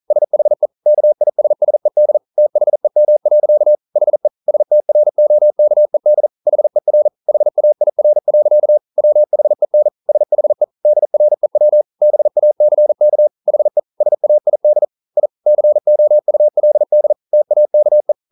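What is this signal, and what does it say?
Morse code, heard as a keyed tone, about 41 wpm, 600 Hz. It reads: SHE KISSED THEM. HE STROKED HER HAIR. WHEN SHE DREW BACK HE SAID I COULD TAKE
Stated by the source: Morse code practice at 41 wpm, from CW med Gnister.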